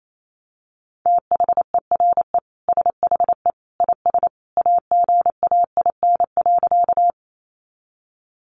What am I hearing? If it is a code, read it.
Morse code, heard as a keyed tone, about 28 wpm, 700 Hz. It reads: THERE SHE IS AGAIN.
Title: Sentences from Top 200 Words 28wpm